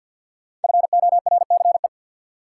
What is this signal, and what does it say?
Morse code, heard as a keyed tone, about 53 wpm, 700 Hz. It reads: FORCE
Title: Sentences from Top 700 Words - New Word Drill - 50wpm